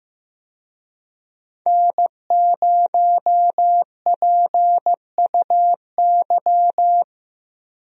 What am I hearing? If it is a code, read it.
Morse code, heard as a keyed tone, about 15 wpm, 700 Hz. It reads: N0PUY
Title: USA Call Signs - 5 Characters - Repeated 2x - 15wpm